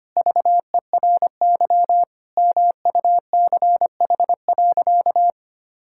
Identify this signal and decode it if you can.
Morse code, heard as a keyed tone, about 25 wpm, 700 Hz. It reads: VERY MUCH.
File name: practice / 25wpm / Sentences with 3-Word Phrases - 25wpm